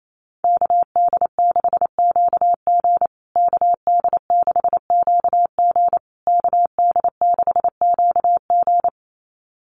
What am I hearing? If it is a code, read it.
Morse code, heard as a keyed tone, about 28 wpm, 700 Hz. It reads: KD6QG KD6QG KD6QG
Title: USA Call Signs Repeated 3x 28wpm